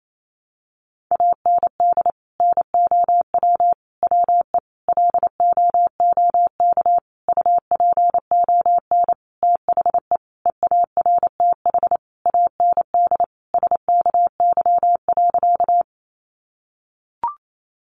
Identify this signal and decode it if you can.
Morse code, heard as a keyed tone, about 28 wpm, 700 Hz. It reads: AND NOW WE LOOK UPON THE EARTH AND SKY.
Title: Sentences from Top 1200 Words 28wpm